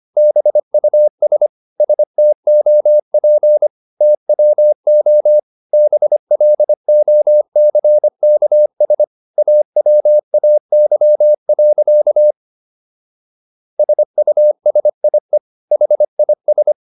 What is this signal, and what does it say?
Morse code, heard as a keyed tone, about 25 wpm, 600 Hz. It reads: BUS STOP TWO BLOCKS AWAY. SUSIE HIS